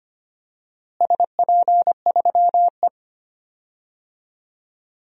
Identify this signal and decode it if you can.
Morse code, heard as a keyed tone, about 25 wpm, 700 Hz. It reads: SP3E
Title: DXCC-100 Call Signs - 4 Characters - Repeated 3x - 25wpm